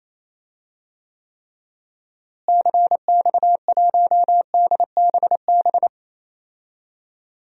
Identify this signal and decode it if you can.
Morse code, heard as a keyed tone, about 28 wpm, 700 Hz. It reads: CX1DBB
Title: DXCC-100 Call Signs - 6 Characters - Repeated 2x - 28wpm